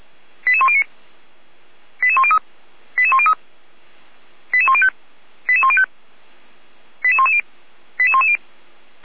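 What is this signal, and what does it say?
Five-tone selective call (ZVEI), heard as quick runs of five tones, tones of about 70 ms. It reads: 88109 88103 88103 88105 88105 88110 88110
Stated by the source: ZVEI five-tone selective call 88104 received by SDR.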